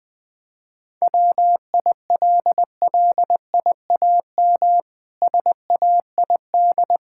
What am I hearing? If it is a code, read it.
Morse code, heard as a keyed tone, about 21 wpm, 700 Hz. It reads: WILLIAM SAID
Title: Pangram Challenge v2 - 20wpm